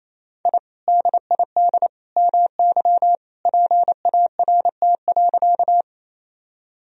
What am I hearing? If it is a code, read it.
Morse code, heard as a keyed tone, about 28 wpm, 700 Hz. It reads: I DID MY PART.
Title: Sentences from Top 1000 Words - Review Part 1 - 28wpm